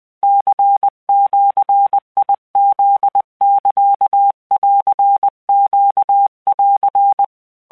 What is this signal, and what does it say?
Morse code (Japanese Wabun code), heard as a keyed tone, about 20 wpm, 800 Hz. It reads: ニジフサンネン